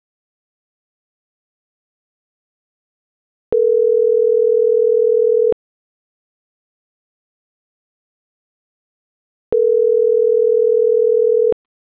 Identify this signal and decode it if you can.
Ringback tone (North America), 440 + 480 Hz.